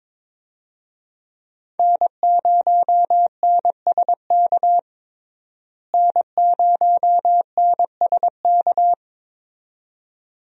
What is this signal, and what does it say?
Morse code, heard as a keyed tone, about 22 wpm, 700 Hz. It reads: N0NSK N0NSK
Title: USA Call Signs - 5 Characters - Repeated 2x - 22wpm